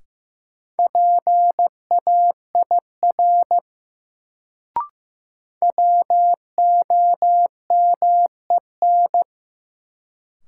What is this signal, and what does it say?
Morse code, heard as a keyed tone, about 15 wpm, 700 Hz. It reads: PAIR WOMEN
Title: Sets of 3 Words - New Word Drill - 15wpm